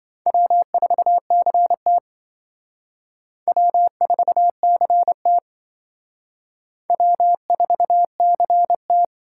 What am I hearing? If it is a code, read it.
Morse code, heard as a keyed tone, about 27 wpm, 700 Hz. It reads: W4CT W4CT W4CT